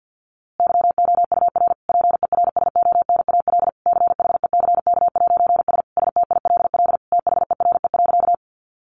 Morse code, heard as a keyed tone, about 50 wpm, 700 Hz. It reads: YOUR PERSONAL CHECK'S STILL THERE.
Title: A Sound Of Thunder 50wpm